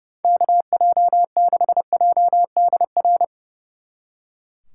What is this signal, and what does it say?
Morse code, heard as a keyed tone, about 38 wpm, 700 Hz. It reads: KJ6JDR